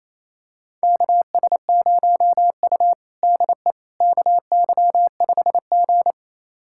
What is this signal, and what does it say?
Morse code, heard as a keyed tone, about 28 wpm, 700 Hz. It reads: KS0U DE KY5G
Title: USA Callsign DE Callsign - 4 Characters - 28wpm